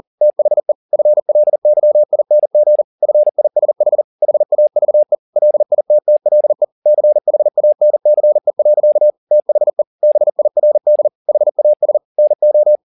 Morse code, heard as a keyed tone, about 40 wpm, 600 Hz. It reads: THE FLYING FISH HAVE LITTLE CHANCE. THE BIRD HAS NO